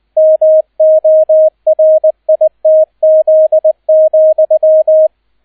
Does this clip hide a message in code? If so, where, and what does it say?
Yes: Morse code from 0.2 to 5.1 s, heard as a keyed tone, about 20 wpm, 630 Hz, with radio noise. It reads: MORITZ,